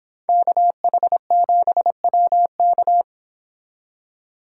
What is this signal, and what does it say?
Morse code, heard as a keyed tone, about 25 wpm, 700 Hz. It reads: KH7WK